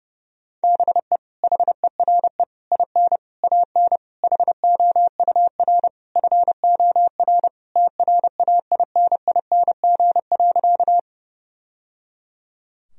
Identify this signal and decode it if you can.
Morse code, heard as a keyed tone, about 30 wpm, 700 Hz. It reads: BE HERE IN AN HOUR FOR TRAINING.
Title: Sentences from Top 1000 Words - Review Part 4 - 30wpm